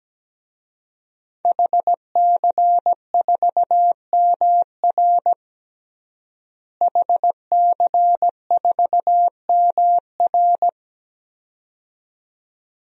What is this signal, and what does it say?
Morse code, heard as a keyed tone, about 17 wpm, 700 Hz. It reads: HC4MR HC4MR
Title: DXCC-100 Call Signs - 5 Characters - Repeated 2x - 17wpm